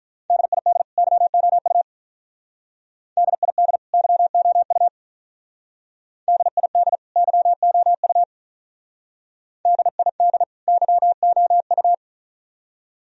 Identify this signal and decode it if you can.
Morse code, heard as a keyed tone, about 46 wpm, 700 Hz. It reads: DID YOU DID YOU DID YOU DID YOU